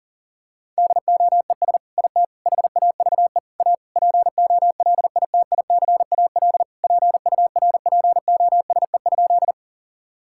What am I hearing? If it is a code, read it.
Morse code, heard as a keyed tone, about 40 wpm, 700 Hz. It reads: DOES IT HAVE A POLITICAL PURPOSE?
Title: Sentences from Top 1000 Words - Review Part 5 - 40wpm